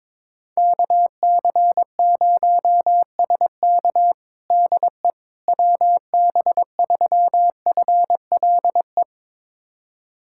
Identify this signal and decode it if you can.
Morse code, heard as a keyed tone, about 21 wpm, 700 Hz. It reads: KC0SK DE WB3FLE
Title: USA Callsign DE Callsign 22wpm